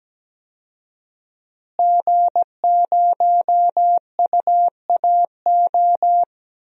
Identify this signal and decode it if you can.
Morse code, heard as a keyed tone, about 17 wpm, 700 Hz. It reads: G0UAO
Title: DXCC-100 Call Signs - 5 Characters - Repeated 2x - 17wpm